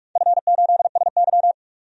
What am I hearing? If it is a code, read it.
Morse code, heard as a keyed tone, about 47 wpm, 700 Hz. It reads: F8SY
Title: DXCC-100 Call Signs - 4 Characters - 45wpm